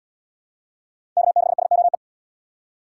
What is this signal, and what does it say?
Morse code, heard as a keyed tone, about 69 wpm, 700 Hz. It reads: K6SOE